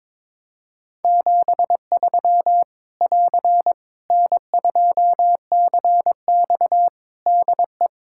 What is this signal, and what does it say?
Morse code, heard as a keyed tone, about 22 wpm, 700 Hz. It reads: 73 <AR> N2CX DE
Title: Practice QSO - Exchange 4 - Ending - 22wpm